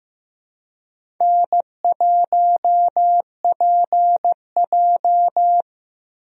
Morse code, heard as a keyed tone, about 16 wpm, 700 Hz. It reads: N1PJ
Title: USA Call Signs - 4 Characters - Repeated 2x - 15wpm